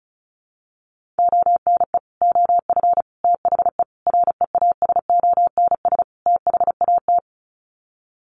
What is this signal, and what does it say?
Morse code, heard as a keyed tone, about 34 wpm, 700 Hz. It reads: ONE OF THE REASONS THAT
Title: Sets of 5 Words Encore 35wpm